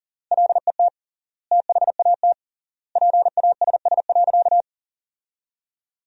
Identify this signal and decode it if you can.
Morse code, heard as a keyed tone, about 37 wpm, 700 Hz. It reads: LET THAT PASS.
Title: 3x Word-Spacing - Sentences from Top 400 Words - 40wpm